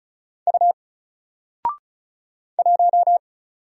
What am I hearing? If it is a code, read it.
Morse code, heard as a keyed tone, about 33 wpm, 700 Hz. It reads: U 1
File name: Single-Letter-Number-35wpm